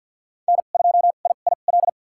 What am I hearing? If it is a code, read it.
Morse code, heard as a keyed tone, about 53 wpm, spaced as for 37 wpm, 700 Hz. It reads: N2IIL
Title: USA Call Signs - 5 Characters - 2x Character Spacing - 50wpm